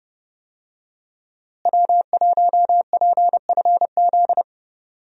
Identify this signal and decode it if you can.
Morse code, heard as a keyed tone, about 30 wpm, 700 Hz. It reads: W1PFZ
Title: USA Call Signs - 5 Characters - Repeated 3x - 30wpm